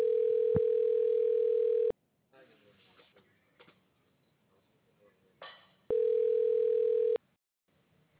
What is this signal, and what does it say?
Ringback tone (North America), 440 + 480 Hz.